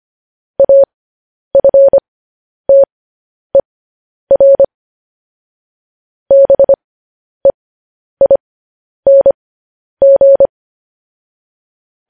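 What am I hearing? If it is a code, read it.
Morse code, heard as a keyed tone, about 25 wpm, spaced as for 10 wpm, 550 Hz. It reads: AFTER BEING
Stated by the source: Morse code news headlines sent at 10 wpm.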